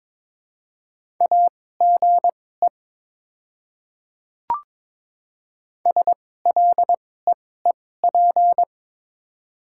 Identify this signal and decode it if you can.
Morse code, heard as a keyed tone, about 22 wpm, spaced as for 16 wpm, 700 Hz. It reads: AGE SLEEP
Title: 2x Character Spacing - Top 500 Words - 22wpm